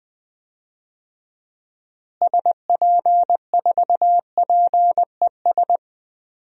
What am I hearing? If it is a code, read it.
Morse code, heard as a keyed tone, about 20 wpm, 700 Hz. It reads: SP4PES
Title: DXCC-100 Call Signs - 6 Characters - Repeated 2x - 20wpm